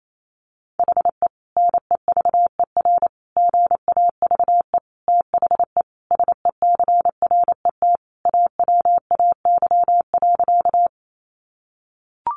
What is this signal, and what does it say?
Morse code, heard as a keyed tone, about 28 wpm, 700 Hz. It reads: HE NEVER GAVE THE SECRET AWAY.
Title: Sentences from Top 1200 Words 28wpm